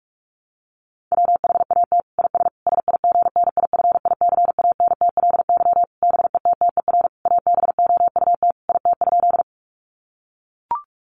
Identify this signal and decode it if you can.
Morse code, heard as a keyed tone, about 44 wpm, 700 Hz. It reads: WHAT IS SIGNIFICANTLY BETTER ABOUT IT?
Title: Sentences from Top 1500 Words 45wpm